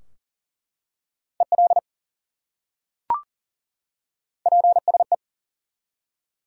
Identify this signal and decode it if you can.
Morse code, heard as a keyed tone, about 39 wpm, 700 Hz. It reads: EL PSE
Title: Top 180 QSO Elements 40wpm